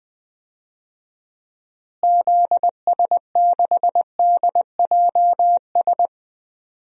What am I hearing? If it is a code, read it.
Morse code, heard as a keyed tone, about 21 wpm, 700 Hz. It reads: ZS6DJS
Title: DXCC-100 Call Signs - 6 Characters - Repeated 2x - 20wpm